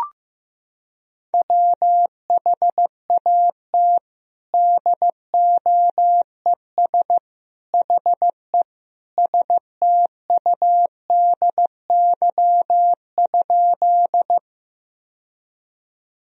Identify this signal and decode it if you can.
Morse code, heard as a keyed tone, about 16 wpm, 700 Hz. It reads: WHAT DOES HE STUDY?
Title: Sentences from Top 1000 Words - Review Part 1 - 15wpm